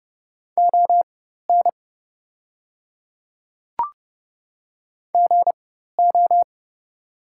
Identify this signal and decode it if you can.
Morse code, heard as a keyed tone, about 30 wpm, spaced as for 14 wpm, 700 Hz. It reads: ON GO
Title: 4x Character Spacing - Top 100 Words - 30wpm